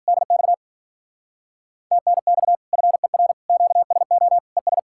Morse code, heard as a keyed tone, about 50 wpm, 700 Hz. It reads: DX TNX FER QSO ES